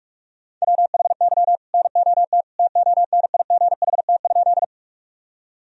Morse code, heard as a keyed tone, about 41 wpm, 700 Hz. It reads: WHY NOT TONIGHT?
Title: Sentences from Top 1000 Words - Review Part 3 - 45wpm